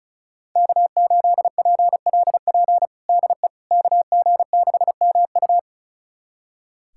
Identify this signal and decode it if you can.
Morse code, heard as a keyed tone, about 36 wpm, 700 Hz. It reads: K8PLP DE KG6MU
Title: USA Callsign DE Callsign - 5 Characters - 35wpm